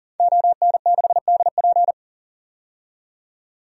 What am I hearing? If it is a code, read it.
Morse code, heard as a keyed tone, about 40 wpm, 700 Hz. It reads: ON6DP
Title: DXCC-100 Call Signs - 5 Characters - 40wpm